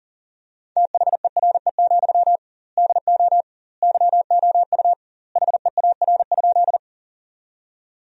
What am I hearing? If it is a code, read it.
Morse code, heard as a keyed tone, about 40 wpm, 700 Hz. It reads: THERE, DO YOU HEAR?
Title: 2x Word-Spacing - Sentences from Top 300 Words - 40wpm